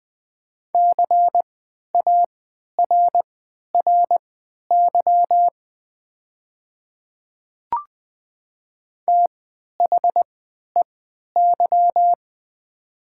Morse code, heard as a keyed tone, about 20 wpm, spaced as for 11 wpm, 700 Hz. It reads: CARRY THEY
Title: 3x Character Spacing - Top 300 Words - 20wpm